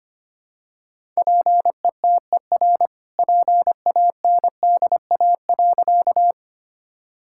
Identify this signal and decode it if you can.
Morse code, heard as a keyed tone, about 25 wpm, 700 Hz. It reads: PETER PANDA.
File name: practice / 25wpm / All-Age Appropriate Jokes - 25wpm